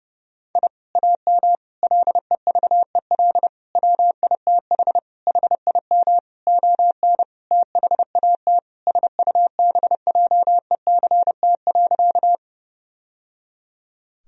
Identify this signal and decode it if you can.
Morse code, heard as a keyed tone, about 32 wpm, 700 Hz. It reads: I AM LEVEL WITH HIM ON THAT SUBJECT.
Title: Sentences from Top 1000 Words - Review Part 4 - 30wpm